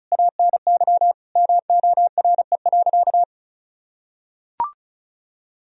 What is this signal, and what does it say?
Morse code, heard as a keyed tone, about 34 wpm, 700 Hz. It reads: ANY MORE.
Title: Sentences from Top 1000 Words - Review Part 1 - 35wpm